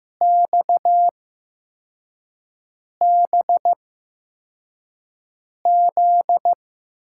Morse code, heard as a keyed tone, about 16 wpm, spaced as for 4 wpm, 700 Hz. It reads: XBZ
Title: USA Call Signs - 5 Characters - 8x Character Spacing - 15wpm